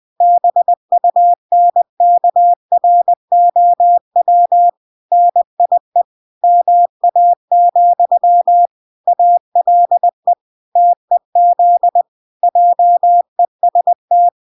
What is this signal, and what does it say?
Morse code, heard as a keyed tone, about 20 wpm, 700 Hz. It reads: BUNKROW NIE MA, ALE TEZ JEST